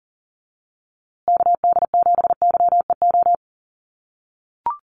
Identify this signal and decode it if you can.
Morse code, heard as a keyed tone, about 40 wpm, 700 Hz. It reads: KD7YEO
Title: USA Call Signs Repeated 2x 40wpm